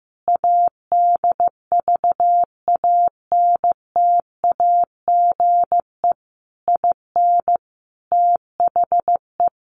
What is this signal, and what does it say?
Morse code, heard as a keyed tone, about 14 wpm, 700 Hz. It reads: ADVANTAGE IN THE
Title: Sentences from Top 900 Words 15wpm